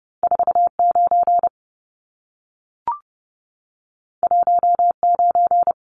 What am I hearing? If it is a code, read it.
Morse code, heard as a keyed tone, about 30 wpm, 700 Hz. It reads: 49 19
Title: Numbers - 2-Digit 30wpm